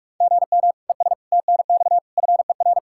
Morse code, heard as a keyed tone, about 46 wpm, 700 Hz. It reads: GM ES TNX FER
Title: K1USN SST QSO Practice 45wpm